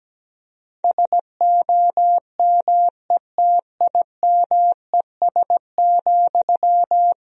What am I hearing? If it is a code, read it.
Morse code, heard as a keyed tone, about 17 wpm, 700 Hz. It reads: SOMETIMES,
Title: Sentences from Top 1000 Words - Review Part 3 - 17wpm